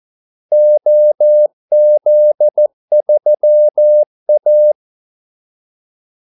Morse code, heard as a keyed tone, about 14 wpm, 600 Hz. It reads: OZ3A
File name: Callsigns-14wpm